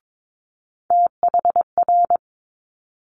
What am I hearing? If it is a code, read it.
Morse code, heard as a keyed tone, about 22 wpm, 700 Hz. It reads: THR
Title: Multi-letter Phonemes Quick Fire 22wpm